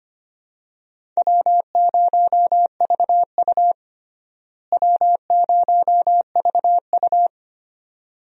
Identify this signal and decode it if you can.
Morse code, heard as a keyed tone, about 25 wpm, 700 Hz. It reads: W0VU W0VU
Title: USA Call Signs - 4 Characters - Repeated 2x - 25wpm